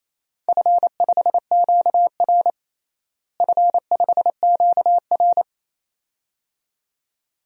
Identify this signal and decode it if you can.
Morse code, heard as a keyed tone, about 28 wpm, 700 Hz. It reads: F5QR F5QR
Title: DXCC-100 Call Signs - 4 Characters - Repeated 2x - 28wpm